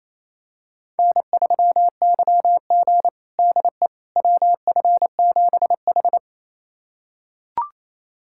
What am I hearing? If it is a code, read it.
Morse code, heard as a keyed tone, about 28 wpm, 700 Hz. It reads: N3YG DE WF7H
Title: USA Callsign DE Callsign - 4 Characters - 28wpm